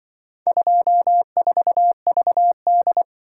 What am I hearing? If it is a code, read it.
Morse code, heard as a keyed tone, about 24 wpm, 700 Hz. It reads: 24VD
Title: Speed-Racing - Four Letter-Number Combinations - 20wpm